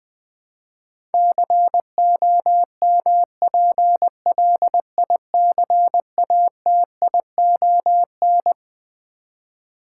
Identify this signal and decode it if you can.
Morse code, heard as a keyed tone, about 21 wpm, 700 Hz. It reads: COMPLICATION
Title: Twelve Letter Words - 20wpm